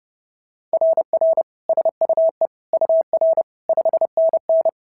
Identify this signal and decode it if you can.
Morse code, heard as a keyed tone, about 30 wpm, 690 Hz. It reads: RR SUE UR 5NN
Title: SOTA QSO Practice 30wpm